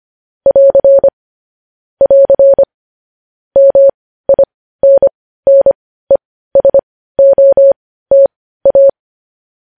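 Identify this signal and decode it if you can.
Morse code, heard as a keyed tone, about 25 wpm, spaced as for 15 wpm, 550 Hz. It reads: <AR> <AR> MINNESOTA
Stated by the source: Morse code news headlines sent at 15.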